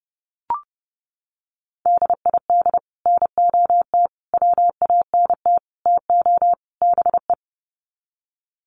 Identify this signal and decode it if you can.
Morse code, heard as a keyed tone, about 30 wpm, 700 Hz. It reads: DID NOT WANT TO BE
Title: Sets of 5 Words 30wpm